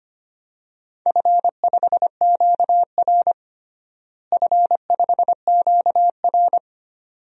Morse code, heard as a keyed tone, about 25 wpm, 700 Hz. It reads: F5QR F5QR
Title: DXCC-100 Call Signs - 4 Characters - Repeated 2x - 25wpm